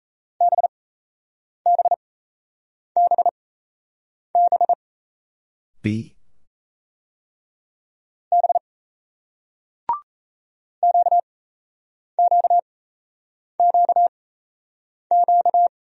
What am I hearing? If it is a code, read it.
Morse code, heard as a keyed tone, about 36 wpm, spaced as for 8 wpm, 700 Hz. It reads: BBBB B QQQQ